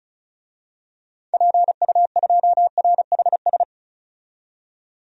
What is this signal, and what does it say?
Morse code, heard as a keyed tone, about 35 wpm, 700 Hz. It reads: PU2RHS
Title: DXCC-100 Call Signs - 6 Characters - 35wpm